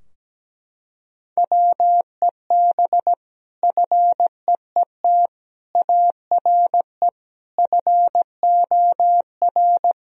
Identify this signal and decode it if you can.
Morse code, heard as a keyed tone, about 17 wpm, 700 Hz. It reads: WEB FEET ARE FOR
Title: Sentences from Top 1000 Words - Review Part 4 - 17wpm